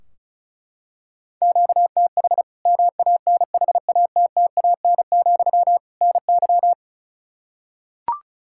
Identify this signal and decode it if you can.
Morse code, heard as a keyed tone, about 36 wpm, 700 Hz. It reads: QTH MANHATTAN, NY
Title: Practice QSO - Exchange 1 - Beginning - 35wpm